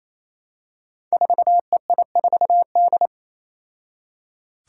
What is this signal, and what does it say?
Morse code, heard as a keyed tone, about 27 wpm, 700 Hz. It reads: 4EI4D
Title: Five Challenging Letter-Number Combinations 28wpm